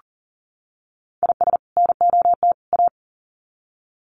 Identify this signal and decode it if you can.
Morse code, heard as a keyed tone, about 38 wpm, 700 Hz. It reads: IS NOT A
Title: Sets of 3 Words 40wpm